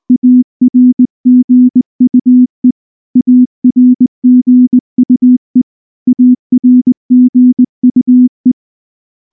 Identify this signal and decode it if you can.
Morse code, heard as a keyed tone, about 19 wpm, 250 Hz. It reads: ARGUE ARGUE ARGUE
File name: morse / morse-training-20WPM-1